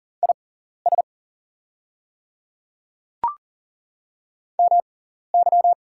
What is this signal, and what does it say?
Morse code, heard as a keyed tone, about 40 wpm, spaced as for 14 wpm, 700 Hz. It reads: IS MY
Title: 6x Character Spacing - Top 200 Words - 40wpm